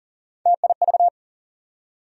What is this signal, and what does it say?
Morse code, heard as a keyed tone, about 40 wpm, 700 Hz. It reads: TIV
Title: USA Call Sign Postfixes - 3 Letters 40wpm